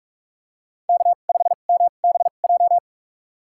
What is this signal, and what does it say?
Morse code, heard as a keyed tone, about 45 wpm, spaced as for 32 wpm, 700 Hz. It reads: K5MBJ
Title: USA Call Signs - 5 Characters - 2x Character Spacing - 45wpm